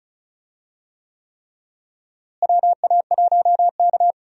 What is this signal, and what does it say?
Morse code, heard as a keyed tone, about 36 wpm, 700 Hz. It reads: WA1K
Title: USA Call Signs - 4 Characters - Repeated 3x - 35wpm